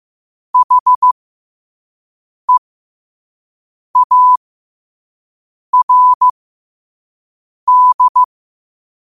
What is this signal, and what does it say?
Morse code, heard as a keyed tone, about 15 wpm, spaced as for 5 wpm, 1000 Hz. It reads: HEARD